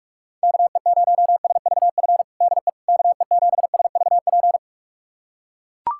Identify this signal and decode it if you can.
Morse code, heard as a keyed tone, about 42 wpm, 700 Hz. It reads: KE0SVF DE KE7SVP
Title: USA Callsign DE Callsign - 6 Characters - 45wpm